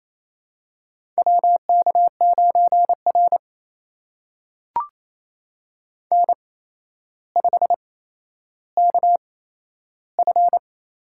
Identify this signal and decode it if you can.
Morse code, heard as a keyed tone, about 28 wpm, spaced as for 12 wpm, 700 Hz. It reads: WK9R N5KF
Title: USA Call Signs - 4 Characters - 8x Character Spacing - 28wpm